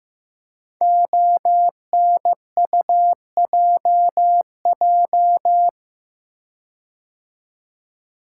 Morse code, heard as a keyed tone, about 15 wpm, 700 Hz. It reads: ONUJJ
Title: Code Groups - 5 Letters - 15wpm